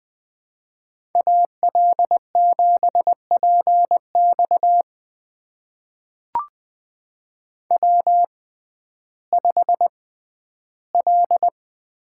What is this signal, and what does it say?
Morse code, heard as a keyed tone, about 20 wpm, spaced as for 12 wpm, 700 Hz. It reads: AL7PX W5L
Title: USA Call Signs - 5 Characters - 6x Character Spacing - 20wpm